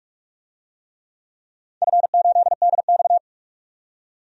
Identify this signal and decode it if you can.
Morse code, heard as a keyed tone, about 46 wpm, 700 Hz. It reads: F8DX
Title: DXCC-100 Call Signs - 4 Characters - 45wpm